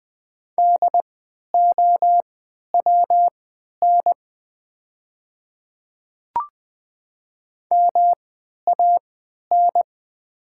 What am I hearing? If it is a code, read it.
Morse code, heard as a keyed tone, about 20 wpm, spaced as for 11 wpm, 700 Hz. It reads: DOWN MAN
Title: 3x Character Spacing - Top 300 Words - 20wpm